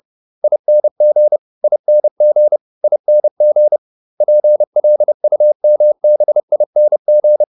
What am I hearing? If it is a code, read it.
Morse code, heard as a keyed tone, about 30 wpm, 600 Hz. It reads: ING ING ING PLUMBING
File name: ING8 30